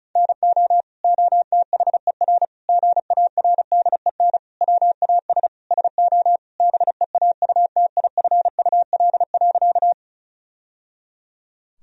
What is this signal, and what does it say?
Morse code, heard as a keyed tone, about 34 wpm, 700 Hz. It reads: NO OTHER GARDEN WAS SO BEAUTIFUL.
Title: Sentences from Top 1000 Words - Review Part 4 - 35wpm